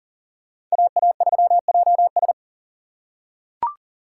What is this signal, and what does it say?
Morse code, heard as a keyed tone, about 40 wpm, 700 Hz. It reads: AA3JS
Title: USA Call Signs - 3x Character Spacing - 40wpm